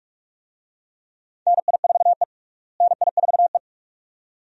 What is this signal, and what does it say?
Morse code, heard as a keyed tone, about 43 wpm, 700 Hz. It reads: NI4E NI4E
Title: USA Call Signs - 4 Characters - Repeated 2x - 45wpm